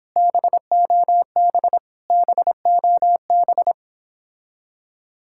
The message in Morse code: BOB BOB

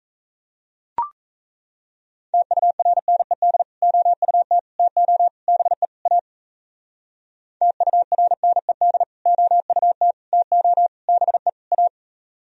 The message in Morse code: TURNED OUT TO BE A TURNED OUT TO BE A

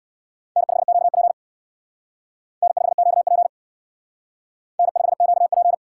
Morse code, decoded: N5QP N5QP N5QP

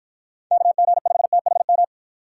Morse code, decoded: KG5THM

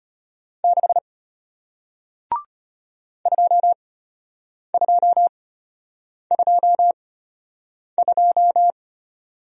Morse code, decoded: 6 2222

